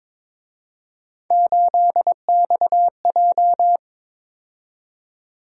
Morse code, heard as a keyed tone, about 26 wpm, 700 Hz. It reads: 8XJ
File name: Speed-Racing - Three Letter-Number Combinations - 22wpm